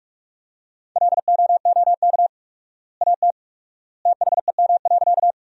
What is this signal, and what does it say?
Morse code, heard as a keyed tone, about 46 wpm, 700 Hz. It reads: LOOK AT THEM.
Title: 4x Word-Spacing - Sentences from Top 100 Words - 45wpm